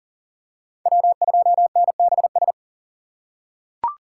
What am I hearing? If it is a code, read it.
Morse code, heard as a keyed tone, about 40 wpm, 700 Hz. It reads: W2NBS
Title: USA Call Signs - 6x Character Spacing - 40wpm